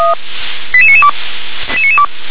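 Five-tone selective call (ZVEI), heard as quick runs of five tones, tones of about 70 ms. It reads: 88992
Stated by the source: ZVEI five-tone selective call received on SDR.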